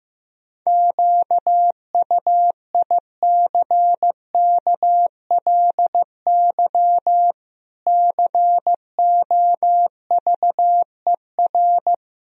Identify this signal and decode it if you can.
Morse code, heard as a keyed tone, about 15 wpm, 700 Hz. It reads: QUICKLY COVER
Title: Sentences from Top 1000 Words - Review Part 1 - 15wpm